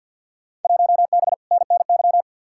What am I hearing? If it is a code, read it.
Morse code, heard as a keyed tone, about 53 wpm, 700 Hz. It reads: 1B NNY
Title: Field Day Exchanges Encore - 50wpm